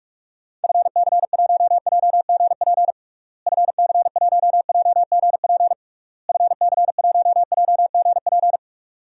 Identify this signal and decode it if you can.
Morse code, heard as a keyed tone, about 45 wpm, 700 Hz. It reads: FC1JGP FC1JGP FC1JGP
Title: DXCC-100 Call Signs - 6 Characters - Repeated 3x - 45wpm